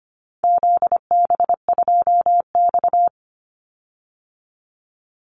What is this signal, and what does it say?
Morse code, heard as a keyed tone, about 25 wpm, 700 Hz. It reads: ZB2X
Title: Call Signs Repeated 2x - 25wpm